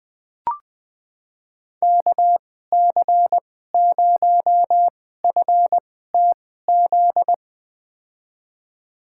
Morse code, heard as a keyed tone, about 20 wpm, spaced as for 14 wpm, 700 Hz. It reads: KC0FTZ
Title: USA Call Signs - 6 Characters - 2x Character Spacing - 20wpm